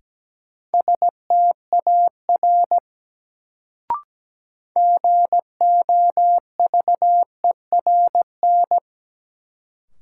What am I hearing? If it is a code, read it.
Morse code, heard as a keyed tone, about 17 wpm, 700 Hz. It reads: STAR GOVERN
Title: Sentences from Top 1000 Words - Review Part 2 - New Word Drill - 17wpm